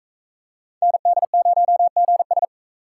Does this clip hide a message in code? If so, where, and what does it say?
Yes: Morse code from 0.8 to 2.4 s, heard as a keyed tone, about 38 wpm, 700 Hz. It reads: ND0GS